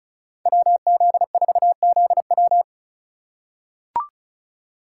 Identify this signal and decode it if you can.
Morse code, heard as a keyed tone, about 36 wpm, 700 Hz. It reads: WZ4ZW
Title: USA Call Signs - 5 Characters - 2x Character Spacing - 35wpm